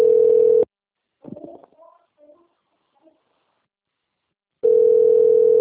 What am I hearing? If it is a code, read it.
Ringback tone (North America), 440 + 480 Hz.